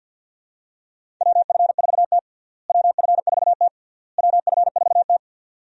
Morse code, heard as a keyed tone, about 52 wpm, 700 Hz. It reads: WF4T WF4T WF4T
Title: USA Call Signs - 4 Characters - Repeated 3x - 50wpm